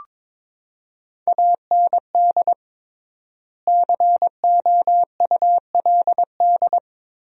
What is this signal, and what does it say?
Morse code, heard as a keyed tone, about 21 wpm, 700 Hz. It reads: AND COULD
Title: 3x Word-Spacing - Sentences from Top 100 Words - 22wpm